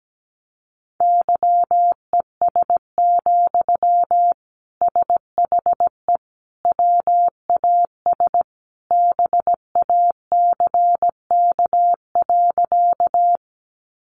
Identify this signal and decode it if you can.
Morse code, heard as a keyed tone, about 17 wpm, 700 Hz. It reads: YES, SHE WAS BACK.
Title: Sentences from Top 500 Words 17wpm